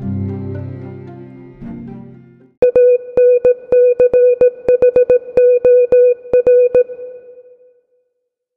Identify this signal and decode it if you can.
Morse code, heard as a keyed tone, about 17 wpm, 470 Hz. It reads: ANCHOR